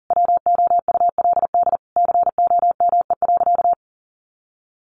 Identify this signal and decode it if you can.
Morse code, heard as a keyed tone, about 40 wpm, 700 Hz. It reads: WOULD COME.
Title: Sentences from Top 1500 Words 40wpm